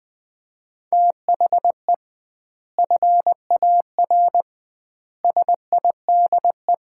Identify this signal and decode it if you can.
Morse code, heard as a keyed tone, about 20 wpm, 700 Hz. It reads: THE FAR SIDE
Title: 2x Word-Spacing - Sets of 5 Words - 20wpm